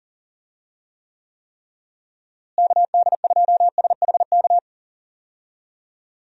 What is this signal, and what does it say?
Morse code, heard as a keyed tone, about 37 wpm, 700 Hz. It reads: KD2SHK
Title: USA Call Signs - 6 Characters - Repeated 3x - 40wpm